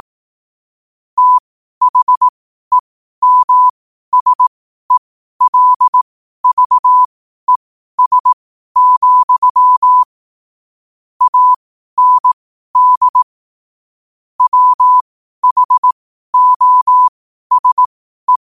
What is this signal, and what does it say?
Morse code, heard as a keyed tone, about 18 wpm, spaced as for 12 wpm, 1000 Hz. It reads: THEMSELVES, AND WHOSE
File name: qotdfiles / 12wpm